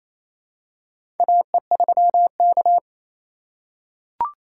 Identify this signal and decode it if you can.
Morse code, heard as a keyed tone, about 28 wpm, 700 Hz. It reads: AE3K
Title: USA Call Signs - 4x Character Spacing - 28wpm